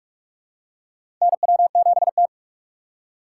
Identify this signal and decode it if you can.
Morse code, heard as a keyed tone, about 44 wpm, 690 Hz. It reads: NW7T
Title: USA Call Signs - 4 Characters - 3x Character Spacing - 45wpm